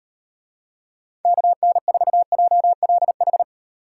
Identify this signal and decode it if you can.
Morse code, heard as a keyed tone, about 38 wpm, 700 Hz. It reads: KN4JLH